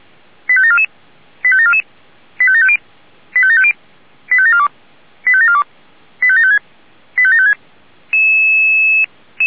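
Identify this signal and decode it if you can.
Five-tone selective call (ZVEI), heard as quick runs of five tones, tones of about 70 ms. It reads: 85744 85744 85750 85750 85742 85742 85756 85756
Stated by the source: ZVEI five-tone call 85749 received by SDR, archived by audio.f85360.feuerwehr-orlamuende.org fire brigade.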